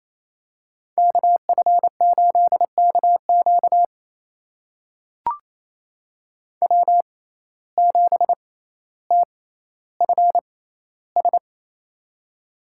Morse code, heard as a keyed tone, about 28 wpm, spaced as for 14 wpm, 700 Hz. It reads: KF8KQ W7TFS
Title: USA Call Signs - 5 Characters - 6x Character Spacing - 28wpm